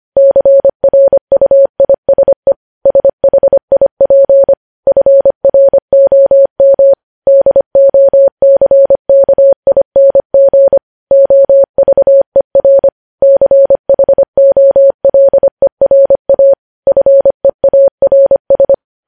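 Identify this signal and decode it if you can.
Morse code, heard as a keyed tone, about 25 wpm, 550 Hz. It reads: CRUISE SHIP FROM DOCKING OVER CHOLERA FEARS
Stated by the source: News headlines sent at 25.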